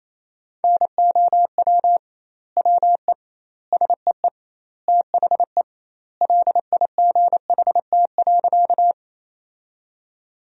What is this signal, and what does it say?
Morse code, heard as a keyed tone, about 28 wpm, 700 Hz. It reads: NOW WE SEE THE LIGHT.
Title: 2x Word-Spacing - Sentences from Top 200 Words - 28wpm